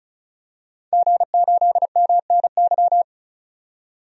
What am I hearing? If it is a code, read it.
Morse code, heard as a keyed tone, about 36 wpm, 700 Hz. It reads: G8MNY
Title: DXCC-100 Call Signs - 5 Characters - 35wpm